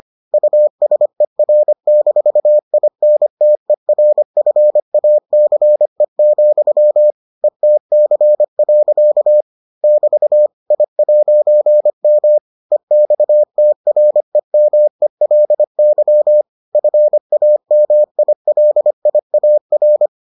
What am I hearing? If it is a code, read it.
Morse code, heard as a keyed tone, about 25 wpm, 600 Hz. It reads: USER-INTERFACE, ETC. <BT> I'M EXTREMELY FAMILIAR